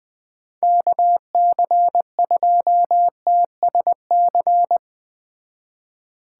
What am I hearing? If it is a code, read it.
Morse code, heard as a keyed tone, about 20 wpm, 700 Hz. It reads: KC2TSC
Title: Speed Racing - USA Call Signs 17wpm